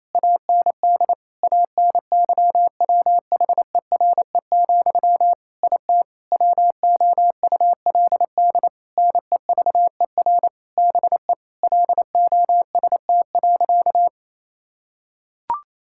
Morse code, heard as a keyed tone, about 28 wpm, 700 Hz. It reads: AND ANYWHERE, IT WOULD NEVER BE LOST.
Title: Sentences from Top 1600 Words 28wpm